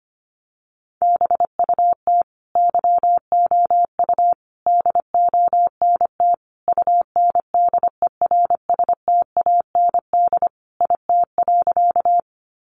Morse code, heard as a keyed tone, about 25 wpm, 700 Hz. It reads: BUT YOU DONT UNDERSTAND IT.
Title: Sentences from Top 600 Words 25wpm